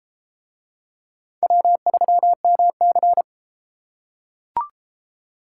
Morse code, heard as a keyed tone, about 32 wpm, 700 Hz. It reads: W3MC